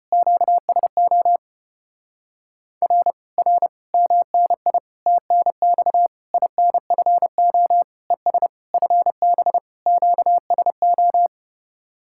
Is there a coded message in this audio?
Yes: Morse code from 0.1 to 11.3 s, heard as a keyed tone, about 33 wpm, 700 Hz. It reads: QSO R R MNI TNX INFO ES FB QSO